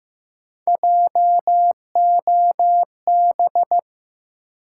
Morse code, heard as a keyed tone, about 15 wpm, 700 Hz. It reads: JOB
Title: Sentences from Top 600 Words - New Word Drill - 15wpm